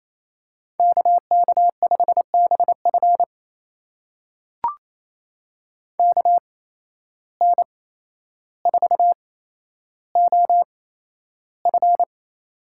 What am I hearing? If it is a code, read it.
Morse code, heard as a keyed tone, about 28 wpm, spaced as for 12 wpm, 700 Hz. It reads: KK5BF KN4OF
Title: USA Call Signs - 5 Characters - 8x Character Spacing - 28wpm